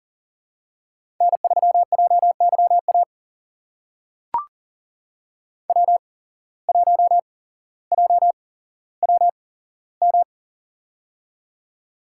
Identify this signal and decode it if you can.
Morse code, heard as a keyed tone, about 40 wpm, spaced as for 17 wpm, 700 Hz. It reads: N3JYA W1JWM
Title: USA Call Signs - 5 Characters - 8x Character Spacing - 40wpm